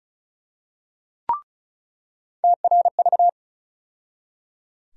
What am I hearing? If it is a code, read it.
Morse code, heard as a keyed tone, about 34 wpm, 700 Hz. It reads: TRV
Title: USA Call Sign Postfixes - 3 Letters 35wpm